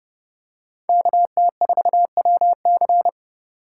Morse code, heard as a keyed tone, about 31 wpm, 700 Hz. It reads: KT4WC